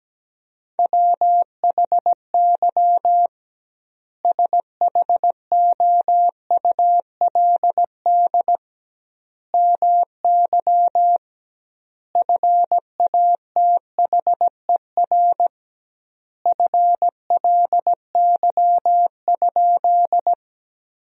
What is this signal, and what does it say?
Morse code, heard as a keyed tone, about 17 wpm, 700 Hz. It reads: WHY SHOULD MY FATHER FLY?
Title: 2x Word-Spacing - Sentences from Top 500 Words - 17wpm